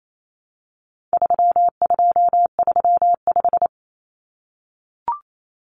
Morse code, heard as a keyed tone, about 28 wpm, 700 Hz. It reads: 3235
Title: Numbers - 4-Digit 28wpm